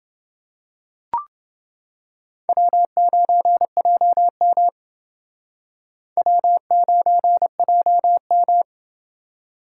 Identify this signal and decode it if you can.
Morse code, heard as a keyed tone, about 28 wpm, 700 Hz. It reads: W9JM W9JM